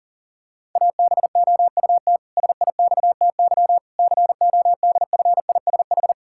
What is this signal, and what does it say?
Morse code, heard as a keyed tone, about 40 wpm, 700 Hz. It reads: ABOUT SIXTY CODFISH
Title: Pangram Challenge v2 - 40wpm